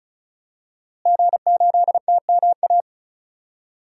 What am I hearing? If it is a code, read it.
Morse code, heard as a keyed tone, about 34 wpm, 700 Hz. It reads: G8TMA
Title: DXCC-100 Call Signs - 5 Characters - Repeated 2x - 35wpm